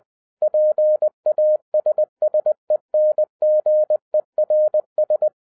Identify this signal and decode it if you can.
Morse code, heard as a keyed tone, about 20 wpm, 610 Hz. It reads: PASSENGERS
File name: 2011-07-mp3 / n18 GREETINGS FROM SINGAPORE 20wpm